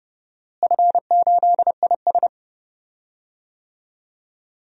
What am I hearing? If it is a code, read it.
Morse code, heard as a keyed tone, about 30 wpm, 700 Hz. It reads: F8IS
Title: DXCC-100 Call Signs - 4 Characters - Repeated 3x - 30wpm